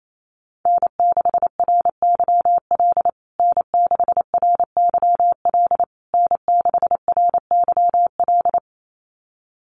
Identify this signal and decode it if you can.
Morse code, heard as a keyed tone, about 28 wpm, 700 Hz. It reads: N6RYL N6RYL N6RYL
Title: USA Call Signs Repeated 3x 28wpm